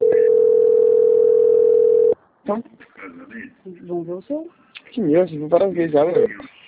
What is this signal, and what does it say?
Ringback tone (North America), 440 + 480 Hz, with speech.
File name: out- - - - - .58862